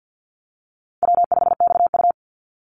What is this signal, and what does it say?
Morse code, heard as a keyed tone, about 48 wpm, 700 Hz. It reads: W5XU
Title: Easy Call Signs Repeated 2x - 50wpm